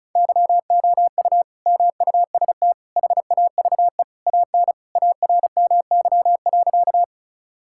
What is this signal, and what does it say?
Morse code, heard as a keyed tone, about 36 wpm, 700 Hz. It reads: YOU MUST HAVE AN ARMY.
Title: Sentences from Top 1000 Words - Review Part 5 - 35wpm